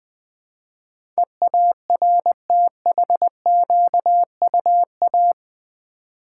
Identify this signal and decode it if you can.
Morse code, heard as a keyed tone, about 20 wpm, 700 Hz. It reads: EARTHQUA
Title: Build A Word - 4 Letters At A Time - 10-Letter Words - 20wpm